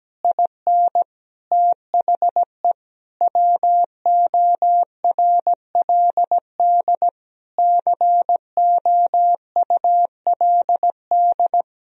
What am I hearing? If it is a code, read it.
Morse code, heard as a keyed tone, about 17 wpm, 700 Hz. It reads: IN THE WORLD COULD
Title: Sentences from Top 1000 Words - Review Part 3 - 17wpm